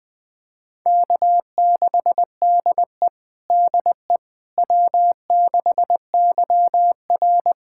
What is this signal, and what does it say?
Morse code, heard as a keyed tone, about 20 wpm, 700 Hz. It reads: K6DE DE W6YR
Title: USA Callsign DE Callsign - 4 Characters - 20wpm